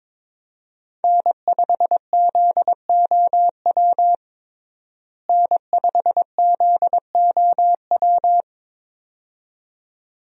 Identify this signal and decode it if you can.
Morse code, heard as a keyed tone, about 22 wpm, 700 Hz. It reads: N5ZOW N5ZOW
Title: USA Call Signs - 5 Characters - Repeated 2x - 22wpm